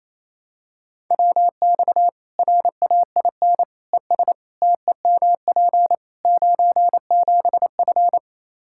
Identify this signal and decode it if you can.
Morse code, heard as a keyed tone, about 28 wpm, 700 Hz. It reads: WX RAIN ES TEMP 97F
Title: Practice QSO - Exchange 2 - Info - 28wpm